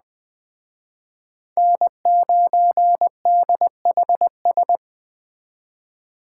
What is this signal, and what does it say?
Morse code, heard as a keyed tone, about 20 wpm, 700 Hz. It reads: N9DHS